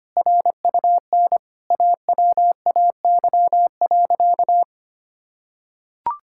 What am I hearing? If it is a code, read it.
Morse code, heard as a keyed tone, about 25 wpm, 700 Hz. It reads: RUN AWAY.
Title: Sentences from Top 1600 Words 25wpm